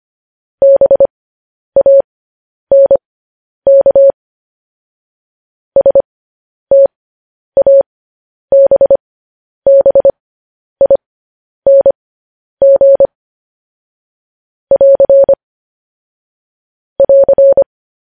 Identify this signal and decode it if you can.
Morse code, heard as a keyed tone, about 25 wpm, spaced as for 10 wpm, 550 Hz. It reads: BANK STABBING <AR> <AR>
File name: podcast / News headlines-1x-25x10